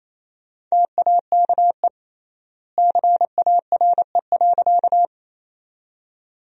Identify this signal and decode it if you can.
Morse code, heard as a keyed tone, about 28 wpm, 700 Hz. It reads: TAKE CARE.
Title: 3x Word-Spacing - Sentences from Top 300 Words - 28wpm